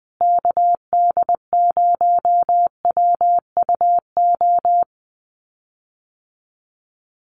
Morse code, heard as a keyed tone, about 20 wpm, 700 Hz. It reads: KD0WUO
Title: USA Call Signs Repeated 3x 20wpm